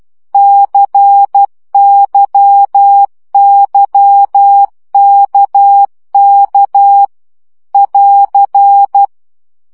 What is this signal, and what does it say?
Morse code, heard as a keyed tone, about 12 wpm, 800 Hz. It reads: CYYKK <AR>
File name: CW gyakorlas k c y 10 baud